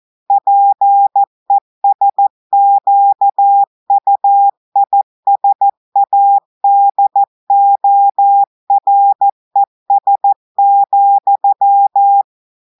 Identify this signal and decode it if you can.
Morse code, heard as a keyed tone, about 14 wpm, 800 Hz. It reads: PESQUISADORES,
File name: audios / portugais / portugais texte en clair 1 vitesse 14 mots minutes